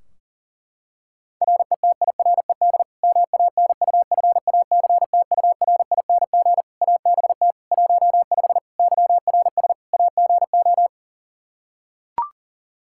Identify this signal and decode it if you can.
Morse code, heard as a keyed tone, about 40 wpm, 700 Hz. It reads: RETIRED MANUFACTURING ABT 15 YRS AGO
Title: Practice QSO - Exchange 3 - More Info - 40wpm